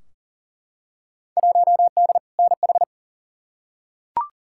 Morse code, heard as a keyed tone, about 40 wpm, 700 Hz. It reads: 1D NH
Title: Field Day Exchanges - 40wpm